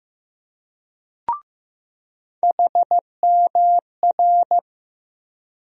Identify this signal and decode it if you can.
Morse code, heard as a keyed tone, about 15 wpm, 700 Hz. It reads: HMR
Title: USA Call Sign Postfixes - 3 Letters 15wpm